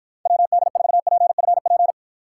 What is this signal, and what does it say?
Morse code, heard as a keyed tone, about 53 wpm, 700 Hz. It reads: WD4PFP